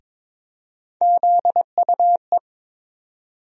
Morse code, heard as a keyed tone, about 22 wpm, 700 Hz. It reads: ZUE
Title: USA Call Sign Postfixes - 3 Letters 22wpm